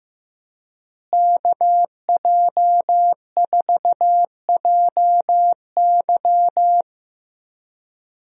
Morse code, heard as a keyed tone, about 15 wpm, 700 Hz. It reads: KJ4JY
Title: USA Call Signs - 5 Characters - Repeated 2x - 15wpm